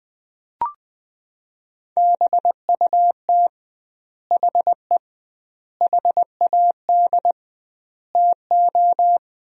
Morse code, heard as a keyed tone, about 21 wpm, 700 Hz. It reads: BUT HE HAD TO
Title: 2x Word-Spacing - Sentences from Top 100 Words - 20wpm